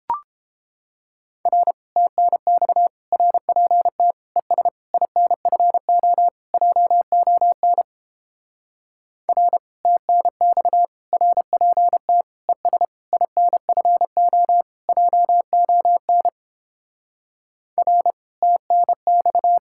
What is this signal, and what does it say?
Morse code, heard as a keyed tone, about 29 wpm, 700 Hz. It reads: R TNX RPT ES INFO JON R TNX RPT ES INFO JON R TNX